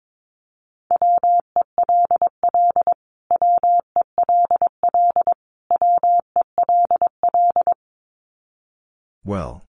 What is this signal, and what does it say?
Morse code, heard as a keyed tone, about 22 wpm, 700 Hz. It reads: WELL WELL WELL
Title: Top 500 Words Repeated 22wpm